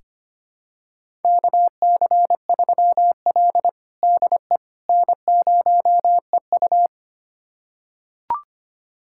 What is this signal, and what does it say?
Morse code, heard as a keyed tone, about 25 wpm, 700 Hz. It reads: KC3L DE N0EU
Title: USA Callsign DE Callsign - 4 Characters - 25wpm